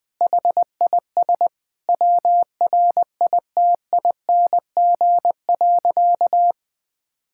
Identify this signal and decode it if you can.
Morse code, heard as a keyed tone, about 20 wpm, 700 Hz. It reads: HIS WRITING.
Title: Sentences with 2-Word Phrases - 20wpm